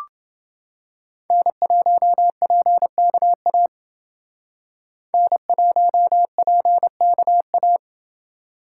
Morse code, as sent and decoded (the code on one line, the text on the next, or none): -. .---- .--. -.- .- / -. .---- .--. -.- .-
N1PKA N1PKA